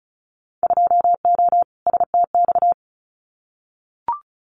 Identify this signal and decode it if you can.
Morse code, heard as a keyed tone, about 34 wpm, 700 Hz. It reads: 2O STX
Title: Winter Field Day Exchanges 35wpm